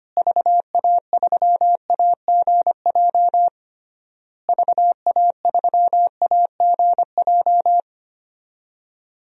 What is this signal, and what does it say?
Morse code, heard as a keyed tone, about 25 wpm, 700 Hz. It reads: VA3AGJ VA3AGJ
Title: DXCC-100 Call Signs - 6 Characters - Repeated 3x - 25wpm